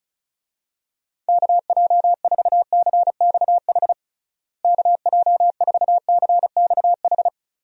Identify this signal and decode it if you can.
Morse code, heard as a keyed tone, about 36 wpm, 700 Hz. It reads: KJ4CXH KJ4CXH
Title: USA Call Signs - 6 Characters - Repeated 3x - 35wpm